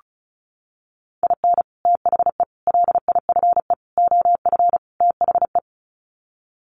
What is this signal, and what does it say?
Morse code, heard as a keyed tone, about 34 wpm, 700 Hz. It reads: IN THE LIFE OF THE
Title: Sets of 5 Words Encore2 35wpm